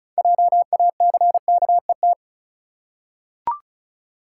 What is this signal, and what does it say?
Morse code, heard as a keyed tone, about 36 wpm, 700 Hz. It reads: JACKET
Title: Six Letter Words - 35wpm